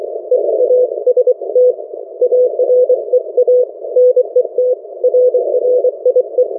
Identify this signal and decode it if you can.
Morse code, heard as a keyed tone, about 24 wpm, 500 Hz. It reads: EAST AREA NET RRIE
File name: RRIE-Net-2mins